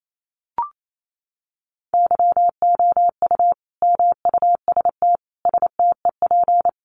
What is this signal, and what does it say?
Morse code, heard as a keyed tone, about 28 wpm, 700 Hz. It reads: YOU MUST STEP
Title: Sentences from Top 400 Words 28wpm